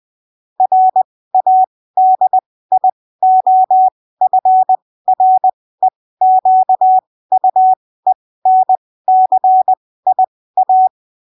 Morse code, heard as a keyed tone, about 20 wpm, spaced as for 15 wpm, 750 Hz. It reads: RADIOFREQUENCIA
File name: arquivos / texto / texto 06 20ppm